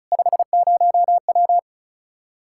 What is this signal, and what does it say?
Morse code, heard as a keyed tone, about 43 wpm, 700 Hz. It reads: 50W
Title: Speed-Racing Common QSO Phrases - 35wpm